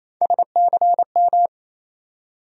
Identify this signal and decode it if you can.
Morse code, heard as a keyed tone, about 28 wpm, 700 Hz. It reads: SCM
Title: USA Call Sign Postfixes - 3 Letters 28wpm